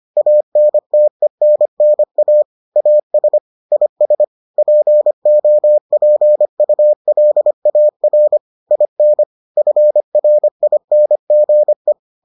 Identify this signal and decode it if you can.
Morse code, heard as a keyed tone, about 25 wpm, 600 Hz. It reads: ANTENNA AS IS POPULAR IN FRINGE